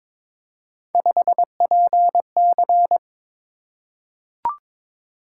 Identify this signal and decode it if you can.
Morse code, heard as a keyed tone, about 22 wpm, 700 Hz. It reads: 5PC
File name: practice / 22wpm / Three Letter-Number Combinations - 22wpm